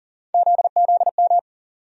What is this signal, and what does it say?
Morse code, heard as a keyed tone, about 40 wpm, 700 Hz. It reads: ZZM
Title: USA Call Sign Postfixes - 3 Letters 40wpm